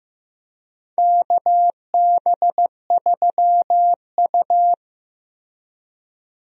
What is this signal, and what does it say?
Morse code, heard as a keyed tone, about 15 wpm, 700 Hz. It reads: KB3U
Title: USA Call Signs - 4 Characters - Repeated 2x - 15wpm